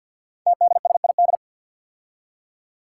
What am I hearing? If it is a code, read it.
Morse code, heard as a keyed tone, about 50 wpm, 700 Hz. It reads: TDSID
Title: Five Challenging Letter-Number Combinations 50wpm